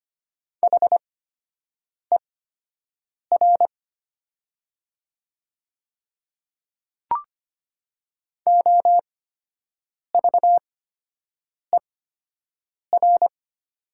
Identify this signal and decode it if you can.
Morse code, heard as a keyed tone, about 25 wpm, spaced as for 7 wpm, 700 Hz. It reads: HER OVER
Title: 8x Character Spacing - Top 300 Words - 25wpm